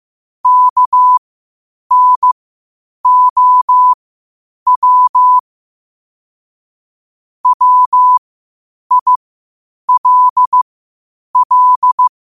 Morse code, heard as a keyed tone, about 15 wpm, spaced as for 8 wpm, 1000 Hz. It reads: KNOW WILL
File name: qotdfiles / 8wpm